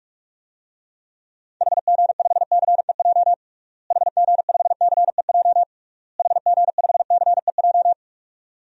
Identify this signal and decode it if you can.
Morse code, heard as a keyed tone, about 41 wpm, 700 Hz. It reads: HG5CEJ HG5CEJ HG5CEJ
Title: DXCC-100 Call Signs - 6 Characters - Repeated 3x - 45wpm